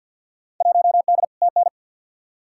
Morse code, heard as a keyed tone, about 53 wpm, 700 Hz. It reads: 1D TN
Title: Field Day Exchanges - 50wpm